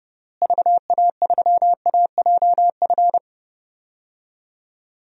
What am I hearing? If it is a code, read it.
Morse code, heard as a keyed tone, about 32 wpm, 700 Hz. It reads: VA3AJF
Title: DXCC-100 Call Signs - 6 Characters - Repeated 3x - 30wpm